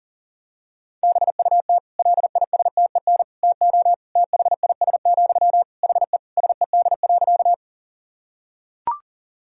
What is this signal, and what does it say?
Morse code, heard as a keyed tone, about 40 wpm, 700 Hz. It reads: BUT LISTEN TO THIS, HE SED.
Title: Sentences from Top 400 Words and Abbreviations 40wpm